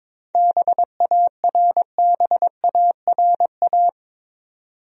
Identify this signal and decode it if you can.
Morse code, heard as a keyed tone, about 22 wpm, 700 Hz. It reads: BARBARA
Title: USA Common Names - 22wpm